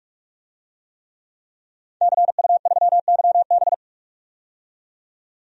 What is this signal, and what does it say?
Morse code, heard as a keyed tone, about 42 wpm, 700 Hz. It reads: CU3YB
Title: DXCC-100 Call Signs - 5 Characters - 45wpm